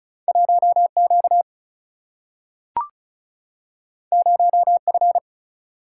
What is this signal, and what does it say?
Morse code, heard as a keyed tone, about 36 wpm, 700 Hz. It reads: 1Q 0F
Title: Single-Letter Single-Number 35wpm